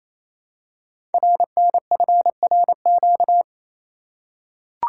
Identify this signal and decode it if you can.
Morse code, heard as a keyed tone, about 28 wpm, 700 Hz. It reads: RNFRQ
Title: Code Groups - 5 Letters - 28wpm